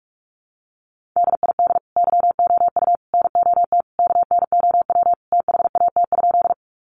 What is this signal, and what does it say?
Morse code, heard as a keyed tone, about 44 wpm, 700 Hz. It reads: DID YOU NOT KNOW THAT?
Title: Sentences from Top 600 Words 45wpm